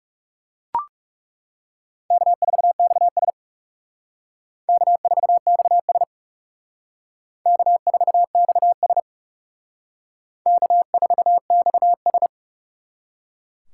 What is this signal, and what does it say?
Morse code, heard as a keyed tone, about 38 wpm, 700 Hz. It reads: K4XS K4XS K4XS K4XS